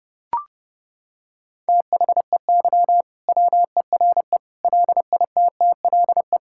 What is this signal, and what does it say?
Morse code, heard as a keyed tone, about 30 wpm, 700 Hz. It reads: THEY WERE LITTLE
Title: Sentences from Top 1600 Words 30wpm